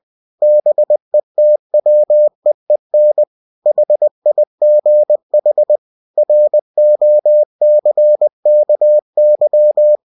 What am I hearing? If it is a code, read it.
Morse code, heard as a keyed tone, about 20 wpm, 600 Hz. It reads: BETWEEN HIGH ROCKY
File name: cw-books / winnie-the-pooh / winnie-20wpm-page-0082